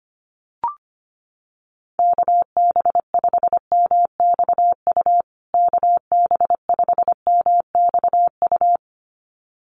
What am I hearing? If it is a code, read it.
Morse code, heard as a keyed tone, about 25 wpm, 700 Hz. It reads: KB5MXU KB5MXU
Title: USA Call Signs Repeated 2x 25wpm